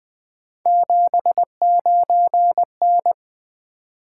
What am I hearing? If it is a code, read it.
Morse code, heard as a keyed tone, about 20 wpm, 700 Hz. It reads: 79N
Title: Three Letter-Number Combinations - 20wpm